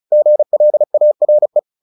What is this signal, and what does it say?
Morse code, heard as a keyed tone, about 34 wpm, 600 Hz. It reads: GLARE
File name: the old man and the sea / the old man and the sea 1 35wpm